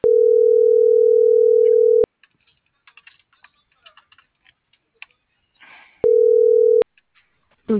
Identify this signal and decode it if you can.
Ringback tone (North America), 440 + 480 Hz.